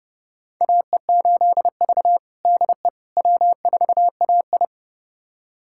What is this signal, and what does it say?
Morse code, heard as a keyed tone, about 30 wpm, 700 Hz. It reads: AE8V DE W4AI
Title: USA Callsign DE Callsign - 4 Characters - 30wpm